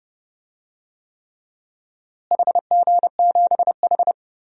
Morse code, heard as a keyed tone, about 29 wpm, 700 Hz. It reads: HG7H
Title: DXCC-100 Call Signs - 4 Characters - Repeated 3x - 30wpm